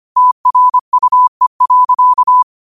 Morse code, heard as a keyed tone, about 25 wpm, 1000 Hz. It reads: TRUE.
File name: qotdfiles / 25wpm